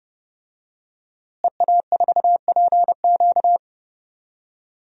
Morse code, heard as a keyed tone, about 30 wpm, 700 Hz. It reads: EA4PQ
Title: DXCC-100 Call Signs - 5 Characters - Repeated 2x - 30wpm